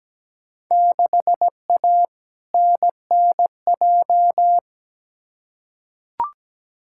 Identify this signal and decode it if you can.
Morse code, heard as a keyed tone, about 17 wpm, 700 Hz. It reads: 6A NNJ
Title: Field Day Exchanges Encore - 17wpm